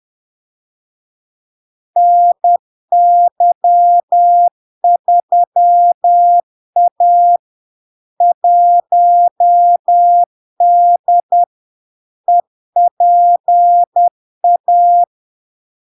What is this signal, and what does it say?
Morse code, heard as a keyed tone, about 10 wpm, 700 Hz. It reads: NY3A 1D EPA
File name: LARK-FD-2022-Log-in-CW-10wpm-700-Hz